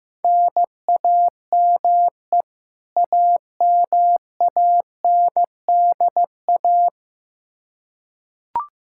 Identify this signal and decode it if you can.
Morse code, heard as a keyed tone, about 15 wpm, 700 Hz. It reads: NAME AMANDA
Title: Practice QSO - Exchange 1 - Beginning - 15wpm